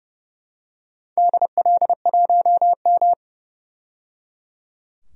Morse code, heard as a keyed tone, about 30 wpm, 700 Hz. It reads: DL1M
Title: DXCC-100 Call Signs - 4 Characters - 30wpm